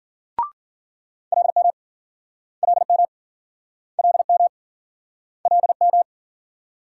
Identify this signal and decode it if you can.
Morse code, heard as a keyed tone, about 57 wpm, 700 Hz. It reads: LM LM LM LM